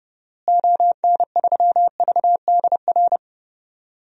Morse code, heard as a keyed tone, about 32 wpm, 700 Hz. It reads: ON3VDR
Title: DXCC-100 Call Signs - 6 Characters - Repeated 2x - 30wpm